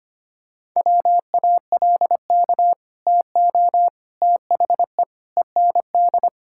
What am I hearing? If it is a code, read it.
Morse code, heard as a keyed tone, about 25 wpm, 700 Hz. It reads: WALK TO THE END
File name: practice / 25wpm / Sentences with 4-Word Phrases - 25wpm